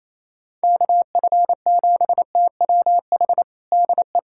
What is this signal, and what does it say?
Morse code, heard as a keyed tone, about 28 wpm, 700 Hz. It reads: KF7TWH DE
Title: USA Callsign DE Callsign - 6 Characters - 28wpm